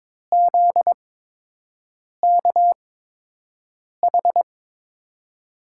Morse code, heard as a keyed tone, about 22 wpm, spaced as for 6 wpm, 700 Hz. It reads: ZKH
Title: USA Call Signs - 5 Characters - 8x Character Spacing - 22wpm